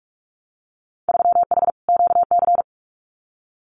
Morse code, heard as a keyed tone, about 44 wpm, 700 Hz. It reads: 3H QC
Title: Winter Field Day Exchanges 45wpm